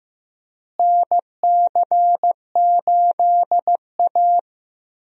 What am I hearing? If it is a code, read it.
Morse code, heard as a keyed tone, about 15 wpm, 700 Hz. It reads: NC8A
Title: USA Call Signs - 4 Characters - Repeated 3x - 15wpm